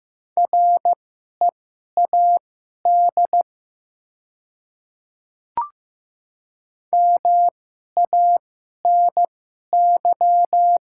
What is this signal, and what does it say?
Morse code, heard as a keyed tone, about 15 wpm, spaced as for 11 wpm, 700 Hz. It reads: READ MANY
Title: 2x Character Spacing - Top 300 Words - 15wpm